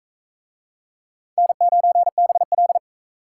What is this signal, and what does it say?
Morse code, heard as a keyed tone, about 42 wpm, 700 Hz. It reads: N9BL